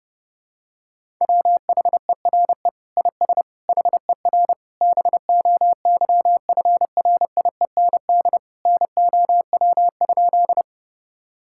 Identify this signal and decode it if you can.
Morse code, heard as a keyed tone, about 30 wpm, 700 Hz. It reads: WHERE IS HER BOYFRIEND NOW?
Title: Sentences from Top 1000 Words - Review Part 5 - 30wpm